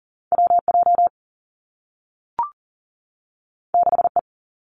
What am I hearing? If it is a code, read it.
Morse code, heard as a keyed tone, about 39 wpm, 700 Hz. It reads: WJ 6E
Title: Two Challenging Letter-Number Combinations 40wpm